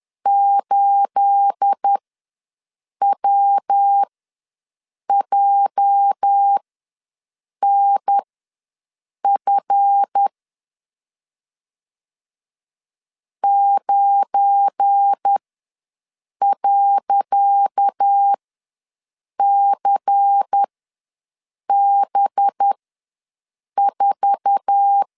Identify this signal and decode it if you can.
Morse code, heard as a keyed tone, about 10 wpm, spaced as for 6 wpm, 800 Hz. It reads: 8WJNF 9.CB4
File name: LEZIONI / 10 SIGNS / 10C DOT ws 10wpm 4213